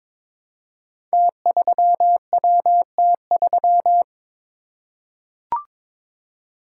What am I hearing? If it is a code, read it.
Morse code, heard as a keyed tone, about 22 wpm, 700 Hz. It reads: T3WT3
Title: Code Groups - 5 Letter:Numbers - 22wpm